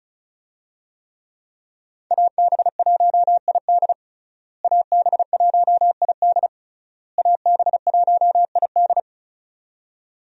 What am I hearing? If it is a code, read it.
Morse code, heard as a keyed tone, about 36 wpm, 700 Hz. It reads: AB1ID AB1ID AB1ID
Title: USA Call Signs - 5 Characters - Repeated 3x - 35wpm